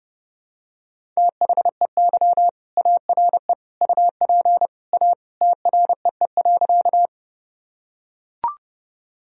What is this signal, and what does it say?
Morse code, heard as a keyed tone, about 32 wpm, 700 Hz. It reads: THEY ARE UP A TREE.
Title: Sentences from Top 1000 Words - Review Part 1 - 30wpm